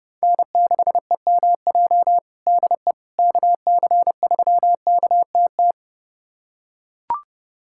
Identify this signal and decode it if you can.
Morse code, heard as a keyed tone, about 30 wpm, 700 Hz. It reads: N6EMJ DE KC3KTT
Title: USA Callsign DE Callsign 30wpm